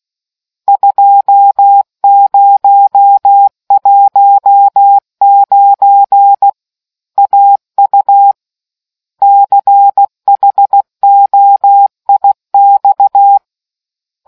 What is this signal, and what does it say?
Morse code, heard as a keyed tone, about 16 wpm, 800 Hz. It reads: 2019 AU CHOIX